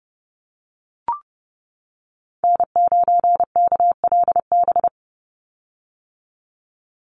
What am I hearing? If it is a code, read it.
Morse code, heard as a keyed tone, about 30 wpm, 700 Hz. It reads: N9KLB
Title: USA Call Signs 30wpm